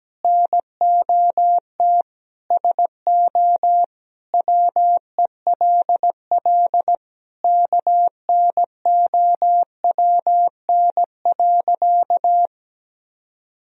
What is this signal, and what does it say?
Morse code, heard as a keyed tone, about 17 wpm, 700 Hz. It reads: NOT SO WELL KNOWN.
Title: Sentences from Top 1000 Words - Review Part 3 - 17wpm